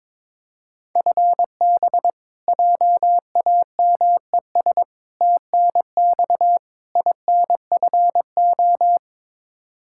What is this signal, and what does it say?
Morse code, heard as a keyed tone, about 21 wpm, 700 Hz. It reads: FB JAMES TNX INFO
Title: Practice QSO - Exchange 3 - More Info - 22wpm